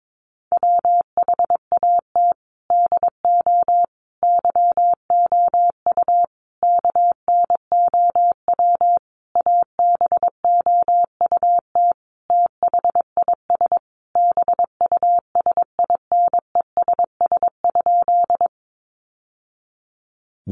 Morse code, heard as a keyed tone, about 22 wpm, 700 Hz. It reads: WHAT DO YOU KNOW ABOUT THIS BUSINESS?
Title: Sentences from Top 600 Words 22wpm